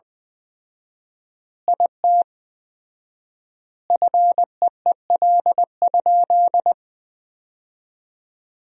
Morse code, heard as a keyed tone, about 20 wpm, 700 Hz. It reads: IT FEEL?
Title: 4x Word-Spacing - Sentences from Top 400 Words - 20wpm